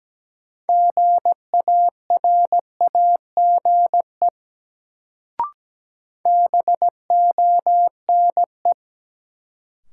Morse code, heard as a keyed tone, about 17 wpm, 700 Hz. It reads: GARAGE BONE
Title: Sentences from Top 1600 Words - New Word Drill Alt - 17wpm